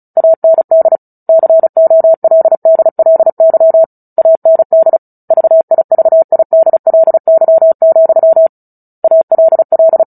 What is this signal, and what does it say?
Morse code, heard as a keyed tone, about 34 wpm, 650 Hz. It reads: AND COLDLY AND VIVIDLY, ALL
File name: Book Two / 09-Chapter 9 - 35 WPM